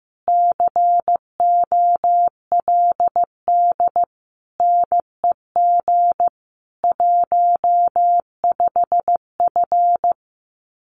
Morse code, heard as a keyed tone, about 15 wpm, 700 Hz. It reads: COLD NEG 15F
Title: Common QSO Phrases Encore 15wpm